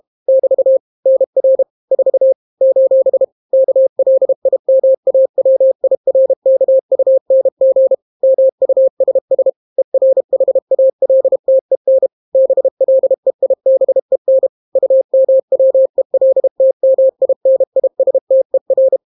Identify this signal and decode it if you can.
Morse code, heard as a keyed tone, about 32 wpm, 530 Hz. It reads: <BT> NR 4 : KLIMAWIRKUNG MUSS ERHALTEN BLEIBEN UMWELTMINISTER